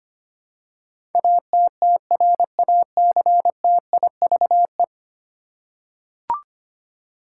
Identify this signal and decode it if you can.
Morse code, heard as a keyed tone, about 25 wpm, 700 Hz. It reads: ATTRACTIVE
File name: 10 Letter Words - First 3 Letters Given - 25wpm